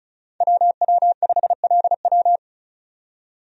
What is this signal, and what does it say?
Morse code, heard as a keyed tone, about 36 wpm, 700 Hz. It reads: WW5LW
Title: USA Call Signs - 5 Characters - 35wpm